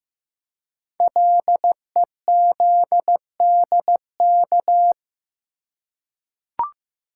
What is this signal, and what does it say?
Morse code, heard as a keyed tone, about 15 wpm, 700 Hz. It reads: LEZDK
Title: Code Groups - 5 Letters - 15wpm